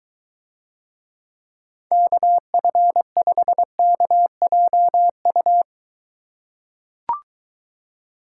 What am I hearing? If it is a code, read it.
Morse code, heard as a keyed tone, about 23 wpm, 700 Hz. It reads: KF5KJU